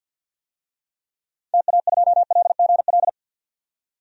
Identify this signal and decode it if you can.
Morse code, heard as a keyed tone, about 51 wpm, 700 Hz. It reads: TA2LGL
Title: DXCC-100 Call Signs - 6 Characters - 50wpm